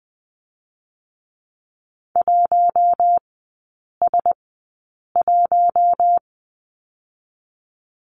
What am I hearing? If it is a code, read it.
Morse code, heard as a keyed tone, about 20 wpm, spaced as for 8 wpm, 700 Hz. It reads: 1S1
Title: 121 Lesson 20e - Number 3 - Three Characters 20wpm